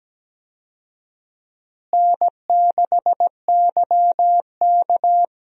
Morse code, heard as a keyed tone, about 17 wpm, 700 Hz. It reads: N6YK
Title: USA Call Signs - 4 Characters - Repeated 3x - 17wpm